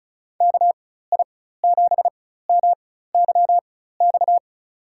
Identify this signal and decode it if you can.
Morse code, heard as a keyed tone, about 36 wpm, spaced as for 16 wpm, 700 Hz. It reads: KI7MYX
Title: USA Call Signs - 6 Characters - 4x Character Spacing - 35wpm